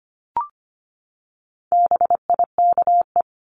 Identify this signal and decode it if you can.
Morse code, heard as a keyed tone, about 25 wpm, 700 Hz. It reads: BIKE
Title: Top 1751-2000 Words 25wpm